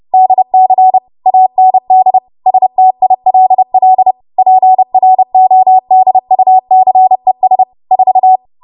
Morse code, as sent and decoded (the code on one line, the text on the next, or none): -.. -.-. / .- -. -.. / ... - .. .-.. .-.. / .--. .-. --- -.. ..- -.-. ... / ....-
DC AND STILL PRODUCES 4